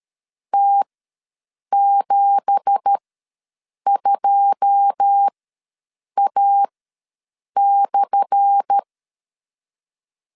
Morse code, heard as a keyed tone, about 13 wpm, spaced as for 7 wpm, 800 Hz. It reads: T72A/
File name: LEZIONI / 10 SIGNS / 10A SLASH ws 12wpm 3443